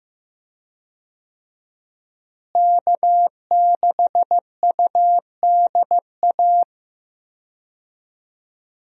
Morse code, heard as a keyed tone, about 15 wpm, 700 Hz. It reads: K6UDA
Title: USA Call Signs - 5 Characters - Repeated 2x - 15wpm